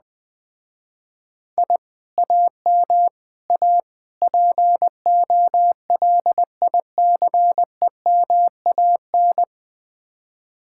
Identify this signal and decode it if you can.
Morse code, heard as a keyed tone, about 20 wpm, 700 Hz. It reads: I AM A POLICEMAN
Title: Speed-Racing Common QSO Phrases - 17wpm